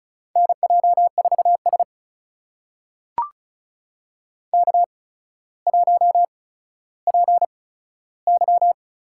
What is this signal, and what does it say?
Morse code, heard as a keyed tone, about 36 wpm, spaced as for 15 wpm, 700 Hz. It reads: NJ4S K1PY
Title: USA Call Signs - 4 Characters - 8x Character Spacing - 35wpm